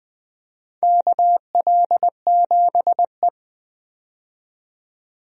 Morse code, heard as a keyed tone, about 20 wpm, 700 Hz. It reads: KL7E
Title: DXCC-100 Call Signs - 4 Characters - 20wpm